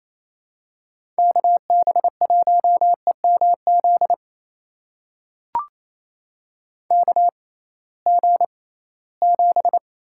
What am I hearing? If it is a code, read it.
Morse code, heard as a keyed tone, about 28 wpm, spaced as for 18 wpm, 700 Hz. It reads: KB1EMZ KG7
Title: USA Call Signs - 6 Characters - 6x Character Spacing - 28wpm